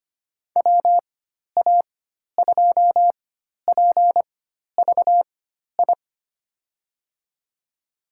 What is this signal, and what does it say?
Morse code, heard as a keyed tone, about 25 wpm, spaced as for 12 wpm, 700 Hz. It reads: WA2PVI
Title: USA Call Signs - 6 Characters - 4x Character Spacing - 25wpm